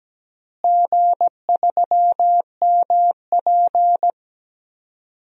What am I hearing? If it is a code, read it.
Morse code, heard as a keyed tone, about 17 wpm, 700 Hz. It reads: G3MP
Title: DXCC-100 Call Signs - 4 Characters - Repeated 2x - 17wpm